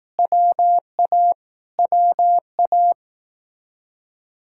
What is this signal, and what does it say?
Morse code, heard as a keyed tone, about 18 wpm, 700 Hz. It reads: WA WA
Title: Speed-Racing Common QSO Phrases Encore - 15wpm